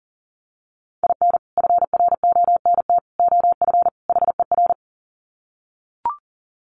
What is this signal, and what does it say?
Morse code, heard as a keyed tone, about 40 wpm, 700 Hz. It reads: IN FRONT OF HER
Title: Sets of 4 Words 40wpm